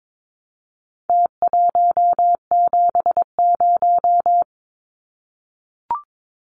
Morse code, heard as a keyed tone, about 22 wpm, 700 Hz. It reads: T170
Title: Numbers - Contest Style - 4-Digit 22wpm